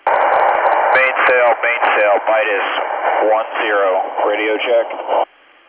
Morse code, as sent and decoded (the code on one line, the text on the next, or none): none